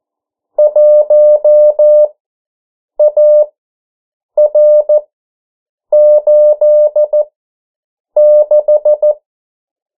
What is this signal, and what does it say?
Morse code, heard as a keyed tone, about 14 wpm, spaced as for 7 wpm, 600 Hz. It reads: 1AR86
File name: morse mp3 / 09 urok 8